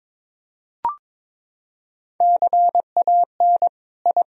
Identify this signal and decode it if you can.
Morse code, heard as a keyed tone, about 22 wpm, 700 Hz. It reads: CAN I
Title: Sentences from Top 1000 Words - Review Part 1 - 22wpm